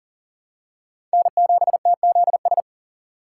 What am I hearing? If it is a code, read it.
Morse code, heard as a keyed tone, about 40 wpm, 700 Hz. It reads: N7TZS
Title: USA Call Signs - 5 Characters - 3x Character Spacing - 40wpm